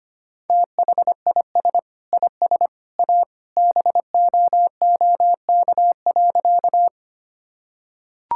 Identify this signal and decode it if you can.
Morse code, heard as a keyed tone, about 25 wpm, 700 Hz. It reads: THIS IS A BOOK.
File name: practice / 25wpm / Sentences with 3-Word Phrases - 25wpm